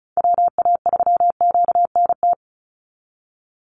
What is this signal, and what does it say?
Morse code, heard as a keyed tone, about 34 wpm, 700 Hz. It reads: WA3QNT
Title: Call Signs Repeated 3x - 35wpm